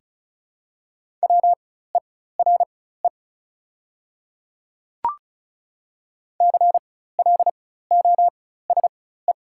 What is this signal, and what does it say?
Morse code, heard as a keyed tone, about 36 wpm, spaced as for 16 wpm, 690 Hz. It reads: WERE CLOSE
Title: 4x Character Spacing - Top 300 Words - 35wpm